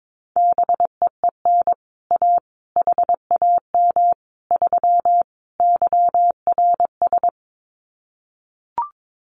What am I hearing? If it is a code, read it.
Morse code, heard as a keyed tone, about 22 wpm, 700 Hz. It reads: BEEN A HAM 3 YRS
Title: Common QSO Phrases Encore 22wpm